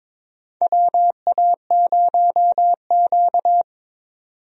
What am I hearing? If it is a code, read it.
Morse code, heard as a keyed tone, about 22 wpm, 700 Hz. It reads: WA0Q
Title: USA Call Signs - 4 Characters - 22wpm